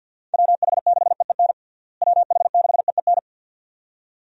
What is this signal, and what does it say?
Morse code, heard as a keyed tone, about 52 wpm, 700 Hz. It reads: WH6EEN WH6EEN
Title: USA Call Signs - 6 Characters - Repeated 2x - 50wpm